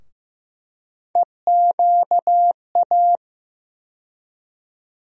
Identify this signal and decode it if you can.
Morse code, heard as a keyed tone, about 15 wpm, 700 Hz. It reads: EQA
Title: USA Call Sign Postfixes - 3 Letters 15wpm